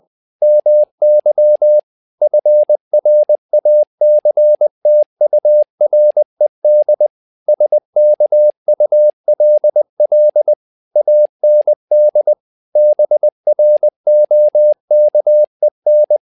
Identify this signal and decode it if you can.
Morse code, heard as a keyed tone, about 20 wpm, 600 Hz. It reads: MY FRACTURED SKULL AND BROKEN
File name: learning center / tower trouble / tower trouble 20wpm